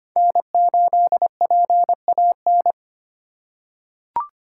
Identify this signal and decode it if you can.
Morse code, heard as a keyed tone, about 25 wpm, 700 Hz. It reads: N8PAN
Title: USA Call Signs - 5 Characters - 3x Character Spacing - 25wpm